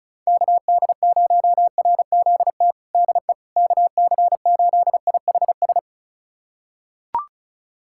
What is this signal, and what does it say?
Morse code, heard as a keyed tone, about 36 wpm, 700 Hz. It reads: KD0RZT DE KC8IHS
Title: USA Callsign DE Callsign 35wpm